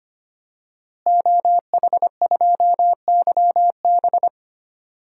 Morse code, heard as a keyed tone, about 25 wpm, 700 Hz. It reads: OH2YB
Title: DXCC-100 Call Signs - 5 Characters - Repeated 2x - 25wpm